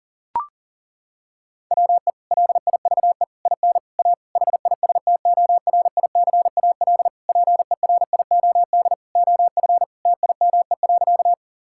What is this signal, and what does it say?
Morse code, heard as a keyed tone, about 40 wpm, 700 Hz. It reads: WE LIVE IN A HISTORICAL PERIOD OF TIME.
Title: Sentences from Top 1000 Words - Review Part 5 - 40wpm